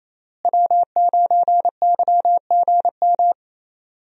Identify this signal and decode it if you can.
Morse code, heard as a keyed tone, about 28 wpm, 700 Hz. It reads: W9YGM
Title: USA Call Signs - 5 Characters - Repeated 3x - 28wpm